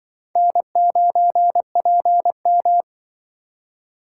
Morse code, heard as a keyed tone, about 28 wpm, 700 Hz. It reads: N9PM